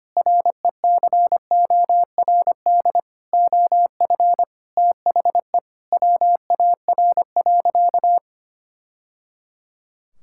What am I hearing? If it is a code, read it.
Morse code, heard as a keyed tone, about 25 wpm, 700 Hz. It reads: RECORD OF THE WAR.
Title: Sentences from Top 1000 Words - Review Part 6 - 25wpm